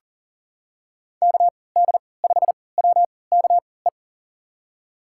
Morse code, heard as a keyed tone, about 40 wpm, spaced as for 23 wpm, 700 Hz. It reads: KD5WKE
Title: USA Call Signs - 6 Characters - 3x Character Spacing - 40wpm